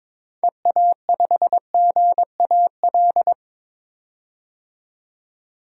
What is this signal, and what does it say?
Morse code, heard as a keyed tone, about 22 wpm, 700 Hz. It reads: EA5GAL
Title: DXCC-100 Call Signs - 6 Characters - 22wpm